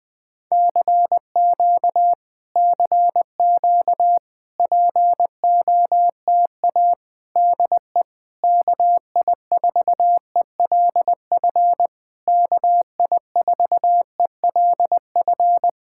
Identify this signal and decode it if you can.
Morse code, heard as a keyed tone, about 20 wpm, 700 Hz. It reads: CQ CQ POTA DE KI4ELF KI4ELF
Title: Speed-Racing Common QSO Phrases - 17wpm